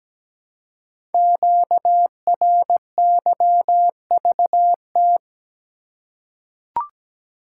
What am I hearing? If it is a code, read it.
Morse code, heard as a keyed tone, about 17 wpm, 700 Hz. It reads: QRYVT
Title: Code Groups - 5 Letter:Numbers - 17wpm